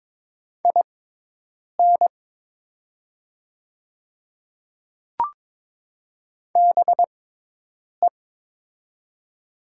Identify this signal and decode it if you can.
Morse code, heard as a keyed tone, about 22 wpm, spaced as for 8 wpm, 700 Hz. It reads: IN BE
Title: 6x Character Spacing - Top 500 Words - 22wpm